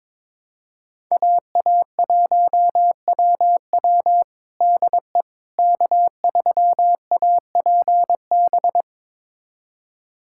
Morse code, heard as a keyed tone, about 22 wpm, 700 Hz. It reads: AA1WW DE K3APB
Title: USA Callsign DE Callsign - 5 Characters - 22wpm